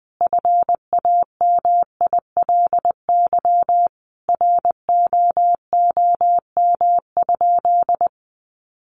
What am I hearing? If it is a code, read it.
Morse code, heard as a keyed tone, about 20 wpm, 700 Hz. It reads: FAMILY ROOM?
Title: Sentences from Top 1200 Words 20wpm